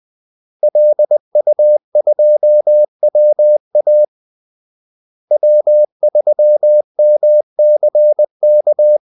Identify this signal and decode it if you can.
Morse code, heard as a keyed tone, about 20 wpm, 600 Hz. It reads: LU2WA W3MCK
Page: Kaldesignaler 20 wpm